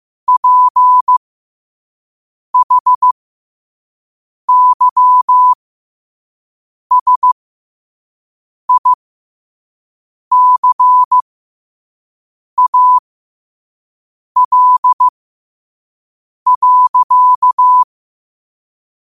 Morse code, heard as a keyed tone, about 15 wpm, spaced as for 5 wpm, 1000 Hz. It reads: PHYSICAL.
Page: Quotes for Thu, 14 Aug 2025 in Morse Code at 5 words per minute.